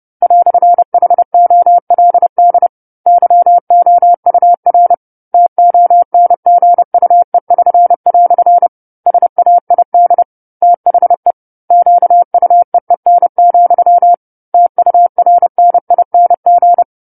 Morse code, heard as a keyed tone, about 30 wpm, 700 Hz. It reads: "HOLD YOUR TONGUE<SN>" SAID THE QUEEN, TURNING
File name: 10-Chapter-10 30 WPM